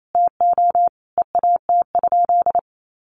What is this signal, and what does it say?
Morse code, heard as a keyed tone, about 28 wpm, 700 Hz. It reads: TO EAT?
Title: Sentences from Top 600 Words 28wpm